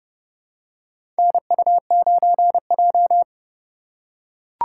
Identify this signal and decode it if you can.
Morse code, heard as a keyed tone, about 32 wpm, 700 Hz. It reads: NU9J
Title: USA Call Signs - 4 Characters - 3x Character Spacing - 30wpm